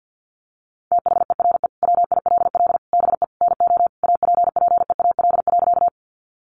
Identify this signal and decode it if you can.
Morse code, heard as a keyed tone, about 50 wpm, 700 Hz. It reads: THERE WILL BE NO APPEAL.
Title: Sentences from Top 1300 Words 50wpm